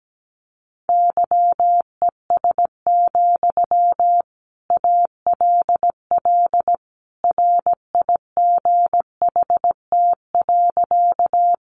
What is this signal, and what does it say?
Morse code, heard as a keyed tone, about 17 wpm, 700 Hz. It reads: YES, ALL RIGHT.
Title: Sentences from Top 500 Words 17wpm